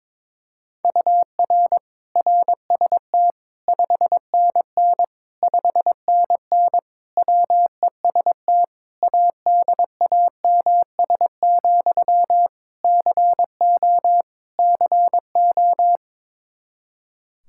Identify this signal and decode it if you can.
Morse code, heard as a keyed tone, about 22 wpm, 700 Hz. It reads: UR RST 5NN 5NN WEST ADAMS, CO CO
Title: Practice QSO - Exchange 1 - Beginning - 22wpm